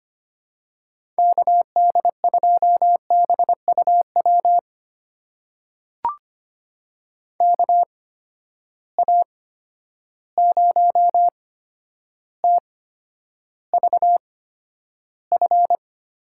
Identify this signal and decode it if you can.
Morse code, heard as a keyed tone, about 25 wpm, spaced as for 11 wpm, 700 Hz. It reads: KD2BUW KA0TVF
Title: USA Call Signs - 6 Characters - 8x Character Spacing - 25wpm